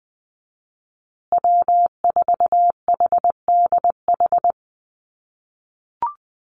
Morse code, heard as a keyed tone, about 20 wpm, 700 Hz. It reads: W4HDH
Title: USA Call Signs Repeated 2x 20wpm